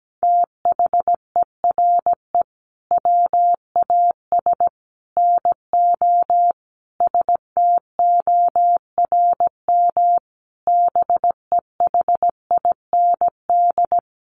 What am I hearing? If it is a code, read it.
Morse code, heard as a keyed tone, about 17 wpm, 700 Hz. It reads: THERE WAS NO STORM BEHIND
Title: Sentences from Top 1500 Words 17wpm